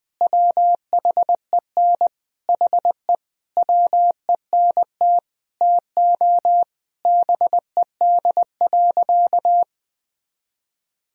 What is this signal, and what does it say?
Morse code, heard as a keyed tone, about 20 wpm, 700 Hz. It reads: WHEN HE WENT TO BED.
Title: Sentences from Top 1000 Words - Review Part 3 - 20wpm